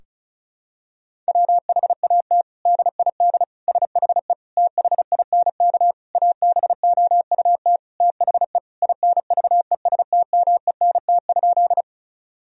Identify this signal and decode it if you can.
Morse code, heard as a keyed tone, about 36 wpm, 700 Hz. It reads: WHAT DID SHE THINK ABOUT THE INVESTMENT?
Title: Sentences from Top 1000 Words - Review Part 4 - 35wpm